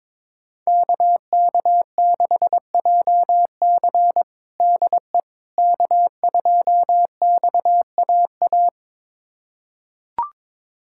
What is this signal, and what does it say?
Morse code, heard as a keyed tone, about 21 wpm, 700 Hz. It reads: KK6JC DE K2XAA
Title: USA Callsign DE Callsign - 5 Characters - 22wpm